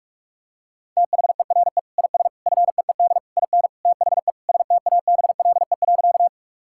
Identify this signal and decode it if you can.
Morse code, heard as a keyed tone, about 46 wpm, 700 Hz. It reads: THERE IS FEED IN THE STABLE.
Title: Sentences from Top 1600 Words 45wpm